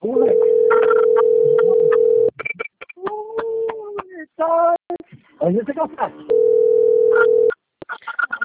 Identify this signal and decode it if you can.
Ringback tone (North America), 440 + 480 Hz, with speech.
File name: out- - - - - .68736